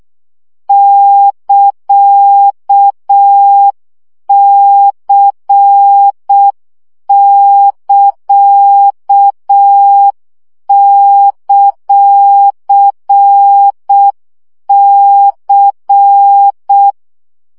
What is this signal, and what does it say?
Morse code, heard as a keyed tone, about 6 wpm, 800 Hz. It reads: <KA>C<KA>;C
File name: CW gyakorlas c ct pontosvesszo 5 baud